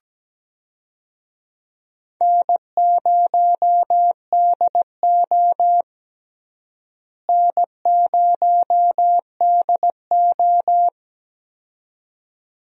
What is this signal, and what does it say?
Morse code, heard as a keyed tone, about 17 wpm, 700 Hz. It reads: N0DO N0DO
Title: USA Call Signs - 4 Characters - Repeated 2x - 17wpm